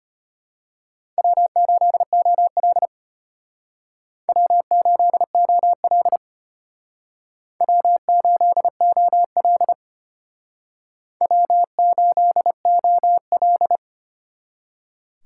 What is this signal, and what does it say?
Morse code, heard as a keyed tone, about 32 wpm, 700 Hz. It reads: W8OL W8OL W8OL W8OL